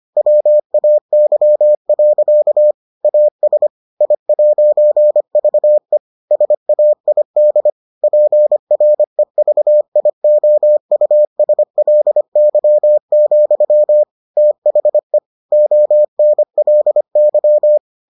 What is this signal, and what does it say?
Morse code, heard as a keyed tone, about 25 wpm, 600 Hz. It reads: WAY. AS I'VE SAID PREVIOUSLY, THE ONLY